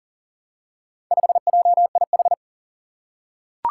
Morse code, heard as a keyed tone, about 40 wpm, 700 Hz. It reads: 5JIH
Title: Four Challenging Letter-Number Combinations 40wpm